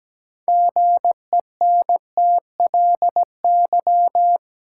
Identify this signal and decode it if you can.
Morse code, heard as a keyed tone, about 17 wpm, 700 Hz. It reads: GENTLY
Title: Sentences from Top 1500 Words - New Word Drill Alt - 17wpm